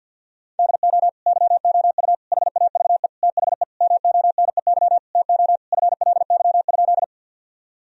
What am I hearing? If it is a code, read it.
Morse code, heard as a keyed tone, about 50 wpm, 700 Hz. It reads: DO YOU HAVE THE MONEY TO FLY?
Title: Sentences from Top 1000 Words - Review Part 2 - 50wpm